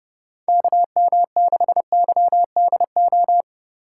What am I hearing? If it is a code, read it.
Morse code, heard as a keyed tone, about 30 wpm, 700 Hz. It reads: KM6YDO
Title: USA Call Signs - 6 Characters - 6x Character Spacing - 30wpm